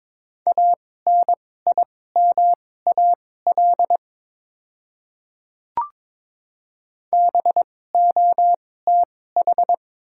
Morse code, heard as a keyed tone, about 21 wpm, spaced as for 16 wpm, 700 Hz. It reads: ANIMAL BOTH
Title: 2x Character Spacing - Top 300 Words - 22wpm